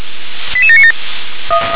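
Five-tone selective call (ZVEI), heard as quick runs of five tones, tones of about 70 ms. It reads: 88797
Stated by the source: Received on SDR, ZVEI five-tone code 88792, 357 channel.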